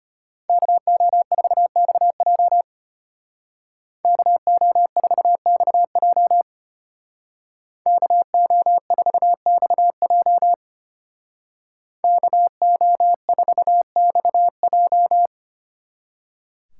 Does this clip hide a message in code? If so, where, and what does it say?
Yes: Morse code from 0.5 to 15.3 s, heard as a keyed tone, about 33 wpm, 700 Hz. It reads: KO4XJ KO4XJ KO4XJ KO4XJ